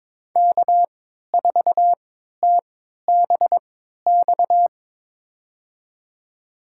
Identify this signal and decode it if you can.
Morse code, heard as a keyed tone, about 22 wpm, spaced as for 13 wpm, 700 Hz. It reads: K4TBX
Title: USA Call Signs - 3x Character Spacing - 22wpm